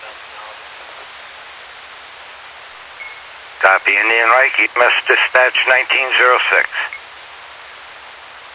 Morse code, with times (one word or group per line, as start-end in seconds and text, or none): none